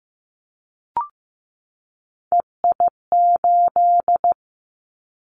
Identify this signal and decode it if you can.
Morse code, heard as a keyed tone, about 15 wpm, 700 Hz. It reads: EI8
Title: Callsign Prefixes for Top 50 DXCC - 15wpm